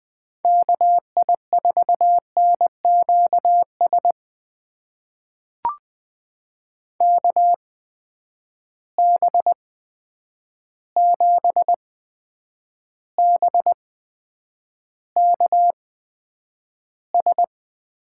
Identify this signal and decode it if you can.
Morse code, heard as a keyed tone, about 20 wpm, spaced as for 9 wpm, 700 Hz. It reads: KI4NQS KB7BKS
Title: USA Call Signs - 6 Characters - 8x Character Spacing - 20wpm